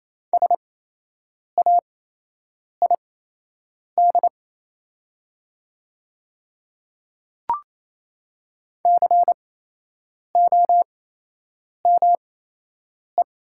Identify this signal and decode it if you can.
Morse code, heard as a keyed tone, about 28 wpm, spaced as for 8 wpm, 700 Hz. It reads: SAID COME
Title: 8x Character Spacing - Top 200 Words - 28wpm